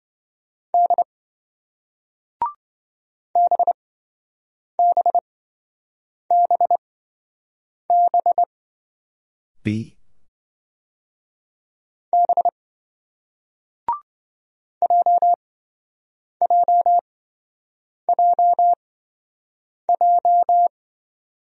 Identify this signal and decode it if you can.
Morse code, heard as a keyed tone, about 26 wpm, spaced as for 7 wpm, 700 Hz. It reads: D BBBB B JJJJ